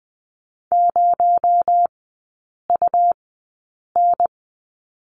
Morse code, heard as a keyed tone, about 20 wpm, spaced as for 8 wpm, 700 Hz. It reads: 0UN
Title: 218 Lesson 35e - Number 0 - Three Characters 20wpm